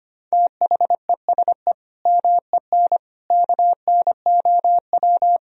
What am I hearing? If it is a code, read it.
Morse code, heard as a keyed tone, about 25 wpm, 700 Hz. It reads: THESE MEN KNOW
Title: Sentences from Top 1000 Words - Review Part 1 - 25wpm